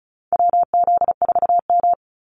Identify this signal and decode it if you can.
Morse code, heard as a keyed tone, about 34 wpm, 700 Hz. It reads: WZ4M
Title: NA SOTA Activator Callsigns 35wpm